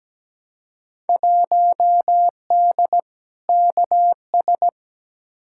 Morse code, heard as a keyed tone, about 17 wpm, 700 Hz. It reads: 1D KS
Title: Field Day Exchanges - 17wpm